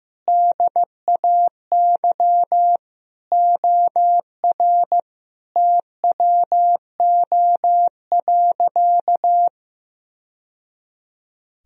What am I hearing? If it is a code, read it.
Morse code, heard as a keyed tone, about 15 wpm, 700 Hz. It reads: DAY OR TWO.
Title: Sentences from Top 1000 Words - Review Part 1 - 15wpm